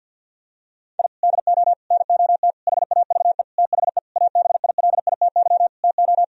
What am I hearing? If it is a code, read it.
Morse code, heard as a keyed tone, about 50 wpm, 700 Hz. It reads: I DO NOT HAVE THE ABILITY TO
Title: Sentences from Top 1000 Words - Review Part 3 - 50wpm